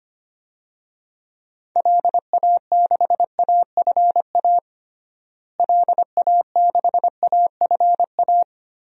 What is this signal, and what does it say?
Morse code, heard as a keyed tone, about 25 wpm, 700 Hz. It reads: LA6AFA LA6AFA
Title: DXCC-100 Call Signs - 6 Characters - Repeated 2x - 25wpm